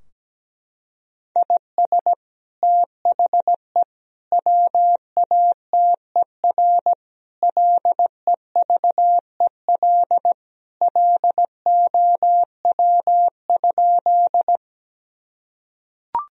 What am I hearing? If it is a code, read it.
Morse code, heard as a keyed tone, about 17 wpm, 700 Hz. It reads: IS THE WATER LEVEL LOW?
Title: Sentences from Top 1000 Words - Review Part 4 - 17wpm